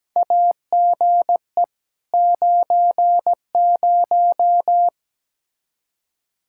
Morse code, heard as a keyed tone, about 17 wpm, 700 Hz. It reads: AGE 90
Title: K1USN SST QSO Practice 17wpm